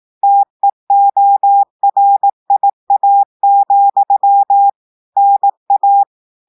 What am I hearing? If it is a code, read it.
Morse code, heard as a keyed tone, about 18 wpm, 800 Hz. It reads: TEORIA, NA
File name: audios / portugais / portugais texte en clair 1 vitesse 18 mots minutes